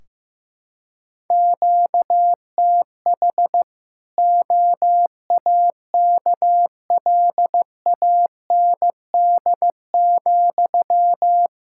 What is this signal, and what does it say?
Morse code, heard as a keyed tone, about 16 wpm, 700 Hz. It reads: QTH OAKLAND,
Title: K1USN SST QSO Practice 15wpm